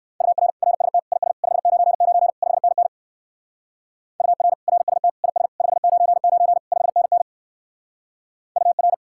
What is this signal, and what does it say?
Morse code, heard as a keyed tone, about 54 wpm, 700 Hz. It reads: UR RST IS 599 5NN UR RST IS 599 5NN UR
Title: Speed-Racing Common QSO Phrases - 45wpm